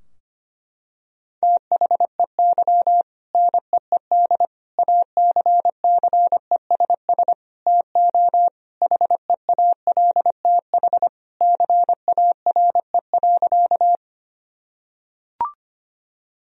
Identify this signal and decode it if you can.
Morse code, heard as a keyed tone, about 25 wpm, 700 Hz. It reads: THEY NEED ACCESS TO HEALTH CARE.
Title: Sentences from Top 1000 Words - Review Part 4 - 25wpm